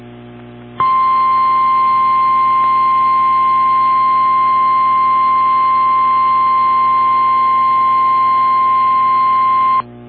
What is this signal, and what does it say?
Emergency Alert System (SAME) tones (the 1050 Hz attention tone). It is heard from 0.7 to 9.9 s.
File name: eas / KWBE / KWBE